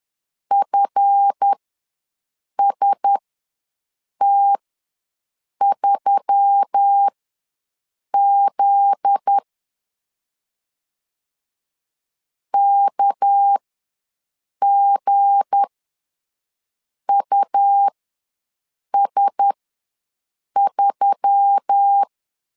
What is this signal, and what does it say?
Morse code, heard as a keyed tone, about 10 wpm, spaced as for 6 wpm, 800 Hz. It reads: FST3Z KGUS3
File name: LEZIONI / 7 GF9Z / 7 GF9Z ws 10wpm 3919